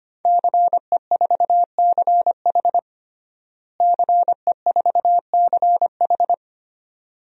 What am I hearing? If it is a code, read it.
Morse code, heard as a keyed tone, about 25 wpm, 700 Hz. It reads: CE4CH CE4CH
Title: DXCC-100 Call Signs - 5 Characters - Repeated 2x - 25wpm